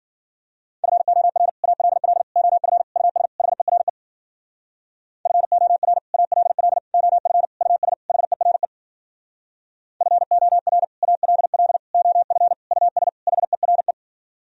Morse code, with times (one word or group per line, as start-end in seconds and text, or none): 0.8-1.5 FOR
1.6-2.2 ALL
2.4-2.8 OF
3.0-3.3 US
3.4-3.9 HERE
5.3-6.0 FOR
6.1-6.8 ALL
6.9-7.4 OF
7.6-7.9 US
8.1-8.7 HERE
10.0-10.8 FOR
11.0-11.8 ALL
11.9-12.5 OF
12.7-13.1 US
13.3-13.9 HERE